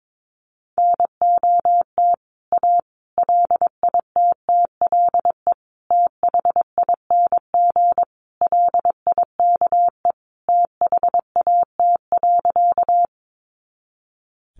Morse code, heard as a keyed tone, about 22 wpm, 700 Hz. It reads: NOT A LITTLE THING LIKE THAT.
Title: A Sound Of Thunder 22wpm